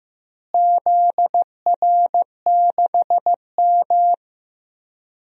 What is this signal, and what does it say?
Morse code, heard as a keyed tone, about 15 wpm, 700 Hz. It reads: ZR6M
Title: DXCC-100 Call Signs - 4 Characters - Repeated 2x - 15wpm